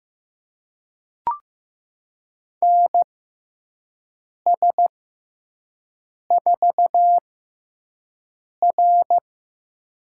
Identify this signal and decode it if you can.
Morse code, heard as a keyed tone, about 15 wpm, spaced as for 5 wpm, 700 Hz. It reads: NS4R
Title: USA Call Signs - 6x Character Spacing - 15wpm